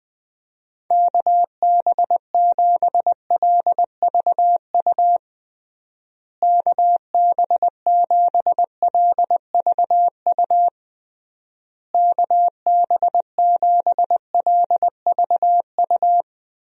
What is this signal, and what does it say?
Morse code, heard as a keyed tone, about 20 wpm, 700 Hz. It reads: KB7LVU KB7LVU KB7LVU
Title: USA Call Signs - 6 Characters - Repeated 3x - 20wpm